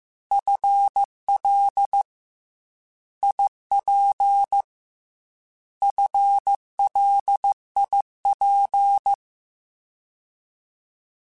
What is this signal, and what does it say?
Morse code, heard as a keyed tone, about 15 wpm, 800 Hz. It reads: FL IP FLIP
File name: FL / FL mp3s / Fast FL @13